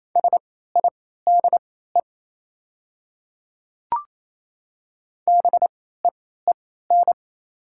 Morse code, heard as a keyed tone, about 27 wpm, spaced as for 16 wpm, 690 Hz. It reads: SIDE BEEN
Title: 3x Character Spacing - Top 500 Words - 28wpm